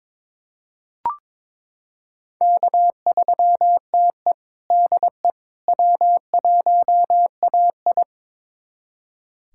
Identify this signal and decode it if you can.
Morse code, heard as a keyed tone, about 22 wpm, 700 Hz. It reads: K3TE DE W1AI
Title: USA Callsign DE Callsign - 4 Characters - 22wpm